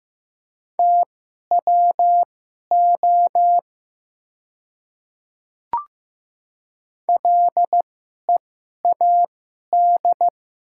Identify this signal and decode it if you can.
Morse code, heard as a keyed tone, about 15 wpm, spaced as for 11 wpm, 700 Hz. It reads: TWO LEAD
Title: 2x Character Spacing - Top 500 Words - 15wpm